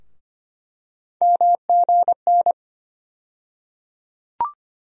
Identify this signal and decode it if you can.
Morse code, heard as a keyed tone, about 25 wpm, 700 Hz. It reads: MGN
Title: USA Call Sign Postfixes - 3 Letters 25wpm